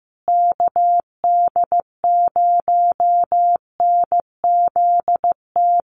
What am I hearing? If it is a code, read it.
Morse code, heard as a keyed tone, about 15 wpm, 700 Hz. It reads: KD0NZT
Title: USA Call Signs Repeated 3x 15wpm